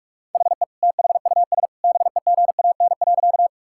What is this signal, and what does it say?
Morse code, heard as a keyed tone, about 41 wpm, 700 Hz. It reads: HE THUS BEGAN.
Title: Sentences from Top 1000 Words - Review Part 5 - 45wpm